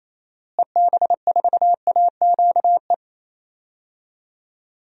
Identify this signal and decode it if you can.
Morse code, heard as a keyed tone, about 28 wpm, 700 Hz. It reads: EB4AQE
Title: DXCC-100 Call Signs - 6 Characters - Repeated 2x - 28wpm